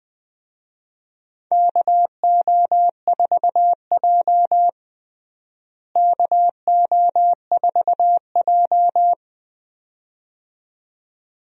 Morse code, heard as a keyed tone, about 20 wpm, 700 Hz. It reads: KO4J KO4J
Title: USA Call Signs - 4 Characters - Repeated 2x - 20wpm